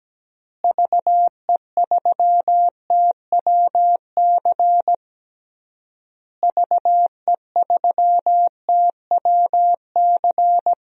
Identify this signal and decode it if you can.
Morse code, heard as a keyed tone, about 17 wpm, 700 Hz. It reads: VE3TWC VE3TWC
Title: DXCC-100 Call Signs - 6 Characters - Repeated 3x - 17wpm